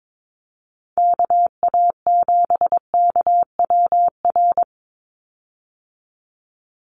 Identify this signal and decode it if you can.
Morse code, heard as a keyed tone, about 22 wpm, 700 Hz. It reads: KA7KWR
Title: USA Call Signs 22wpm